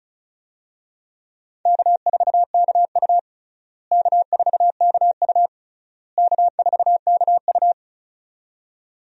Morse code, heard as a keyed tone, about 36 wpm, 700 Hz. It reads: K4KU K4KU K4KU
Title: USA Call Signs - 4 Characters - Repeated 3x - 35wpm